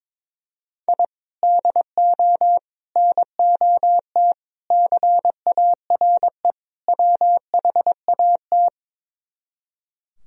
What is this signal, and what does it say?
Morse code, heard as a keyed tone, about 23 wpm, 700 Hz. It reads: I DO NOT CARE WHAT
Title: Speed-Racing - Sets of 5 Words 22wpm